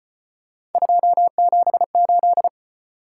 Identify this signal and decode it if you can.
Morse code, heard as a keyed tone, about 35 wpm, 700 Hz. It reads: 278